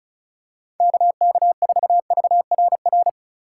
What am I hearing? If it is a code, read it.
Morse code, heard as a keyed tone, about 36 wpm, 700 Hz. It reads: KK4VRR
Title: USA Call Signs - 3x Character Spacing - 35wpm